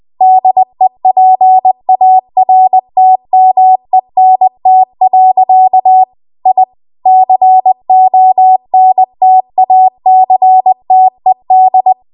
Morse code, heard as a keyed tone, about 20 wpm, 750 Hz. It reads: DEPARTMENT. I CONTACTED
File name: Archive / 20 WPM / 20WPM